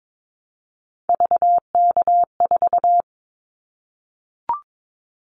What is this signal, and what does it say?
Morse code, heard as a keyed tone, about 22 wpm, 700 Hz. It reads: VK4
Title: Callsign Prefixes for Top 100 DXCC - 22wpm